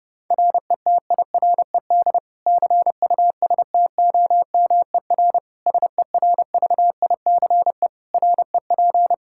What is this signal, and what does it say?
Morse code, heard as a keyed tone, about 32 wpm, 700 Hz. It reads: RETIRED CUSTOMER SERVICE REP
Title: Practice QSO - Exchange 3 - More Info - 30wpm